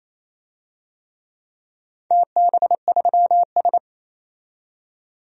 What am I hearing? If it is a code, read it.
Morse code, heard as a keyed tone, about 29 wpm, 700 Hz. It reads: TB3S
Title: DXCC-100 Call Signs - 4 Characters - 28wpm